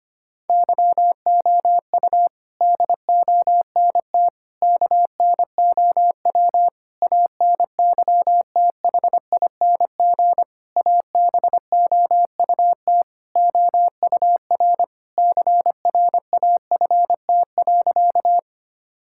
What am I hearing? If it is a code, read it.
Morse code, heard as a keyed tone, about 25 wpm, 700 Hz. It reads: YOU DONT KNOW ANYTHING ABOUT OUR CRAFT.
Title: Sentences from Top 1000 Words - Review Part 6 - 25wpm